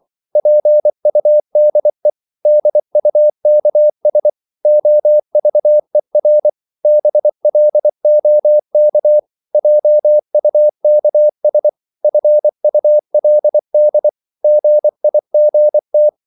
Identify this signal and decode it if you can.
Morse code, heard as a keyed tone, about 24 wpm, 600 Hz. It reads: PUDE DUKS OVER BLOK JUKS FULD GIGT
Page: Korte danske ord 24wpm